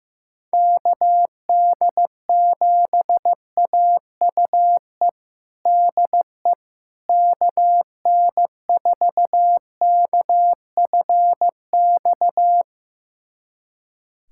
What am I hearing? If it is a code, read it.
Morse code, heard as a keyed tone, about 15 wpm, 700 Hz. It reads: KD7AUE DE KN4KFX
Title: USA Callsign DE Callsign - 6 Characters - 15wpm